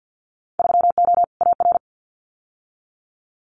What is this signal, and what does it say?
Morse code, heard as a keyed tone, about 50 wpm, 700 Hz. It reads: 3O AR
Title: Winter Field Day Exchanges Encore 50wpm